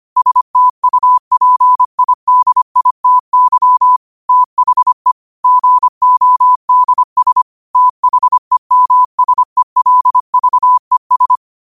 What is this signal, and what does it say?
Morse code, heard as a keyed tone, about 25 wpm, 1000 Hz. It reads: STUPIDITY THE GODS THEMSELVES